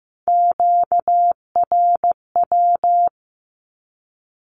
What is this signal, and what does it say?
Morse code, heard as a keyed tone, about 15 wpm, 700 Hz. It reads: QRW
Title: Q-Codes with Spoken Meaning - 15wpm